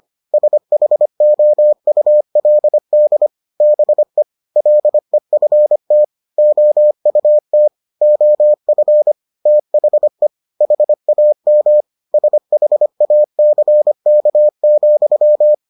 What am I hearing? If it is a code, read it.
Morse code, heard as a keyed tone, about 25 wpm, 600 Hz. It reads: SHOULD BE LEFT OUT OF THE HAM SHACK,